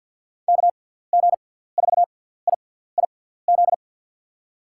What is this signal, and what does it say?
Morse code, heard as a keyed tone, about 47 wpm, spaced as for 17 wpm, 700 Hz. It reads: KG4IIZ
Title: USA Call Signs - 6 Characters - 6x Character Spacing - 50wpm